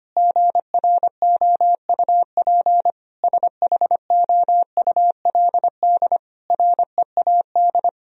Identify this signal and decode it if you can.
Morse code, heard as a keyed tone, about 25 wpm, 700 Hz. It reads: GROUP SHOULD READ
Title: Sentences from Top 1000 Words - Review Part 1 - 25wpm